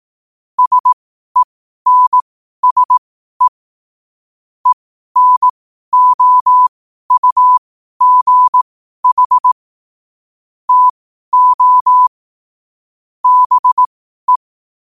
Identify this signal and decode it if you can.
Morse code, heard as a keyed tone, about 18 wpm, spaced as for 12 wpm, 1000 Hz. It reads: SENSE ENOUGH TO BE